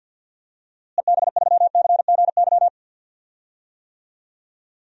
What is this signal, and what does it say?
Morse code, heard as a keyed tone, about 52 wpm, 700 Hz. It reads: EB3CGY
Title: DXCC-100 Call Signs - 6 Characters - Repeated 3x - 50wpm